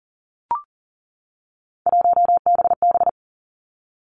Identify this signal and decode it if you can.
Morse code, heard as a keyed tone, about 38 wpm, 700 Hz. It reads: 1BB
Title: Three Challenging Letter-Number Combinations 40wpm